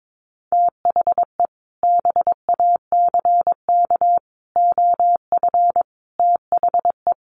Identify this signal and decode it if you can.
Morse code, heard as a keyed tone, about 22 wpm, 700 Hz. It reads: THE BACK OF THE
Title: Sets of 5 Words Encore 22wpm